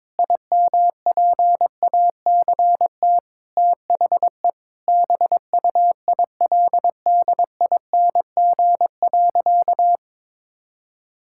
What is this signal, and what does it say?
Morse code, heard as a keyed tone, about 21 wpm, 700 Hz. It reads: IMPACT THE BUILDING.
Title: Sentences from Top 1000 Words - Review Part 6 - 22wpm